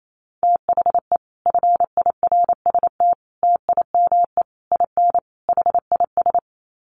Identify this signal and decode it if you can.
Morse code, heard as a keyed tone, about 28 wpm, 700 Hz. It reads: THE FIRST TIME IN HIS
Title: Sets of 5 Words Encore2 28wpm